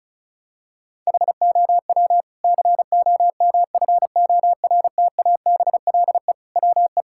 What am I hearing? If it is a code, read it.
Morse code, heard as a keyed tone, about 36 wpm, 700 Hz. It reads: HOW COMFORTABLE WE
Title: Sentences from Top 1600 Words 35wpm